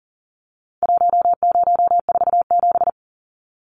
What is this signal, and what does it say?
Morse code, heard as a keyed tone, about 40 wpm, 700 Hz. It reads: 1047
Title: Numbers - Contest Style - 4-Digit 40wpm